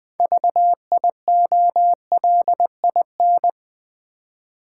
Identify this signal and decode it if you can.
Morse code, heard as a keyed tone, about 20 wpm, 700 Hz. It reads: VIOLIN
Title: Easy Words with Hints - 20wpm